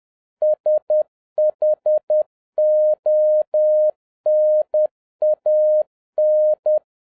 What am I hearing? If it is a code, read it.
Morse code, heard as a keyed tone, about 10 wpm, 610 Hz. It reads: SHONAN
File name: 2011-07-mp3 / n31 THE 3.11 2 10wpm